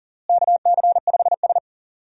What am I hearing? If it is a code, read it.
Morse code, heard as a keyed tone, about 40 wpm, 700 Hz. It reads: KC5S
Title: USA Call Signs - 4 Characters - Repeated 2x - 40wpm